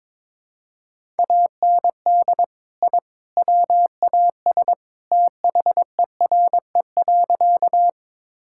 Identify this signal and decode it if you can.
Morse code, heard as a keyed tone, about 22 wpm, 700 Hz. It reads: AND I WAS THERE.
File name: practice / 22wpm / Sentences with 3-Word Phrases - 22wpm